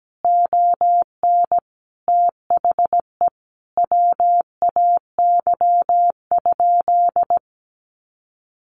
Morse code, heard as a keyed tone, about 17 wpm, 700 Hz. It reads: ON THE WAY?
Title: Sentences from Top 1100 Words 17wpm